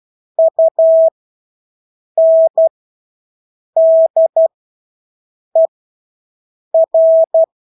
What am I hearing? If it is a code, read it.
Morse code, heard as a keyed tone, about 12 wpm, spaced as for 6 wpm, 650 Hz. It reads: UNDER
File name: Engelske ord 12-6 wpm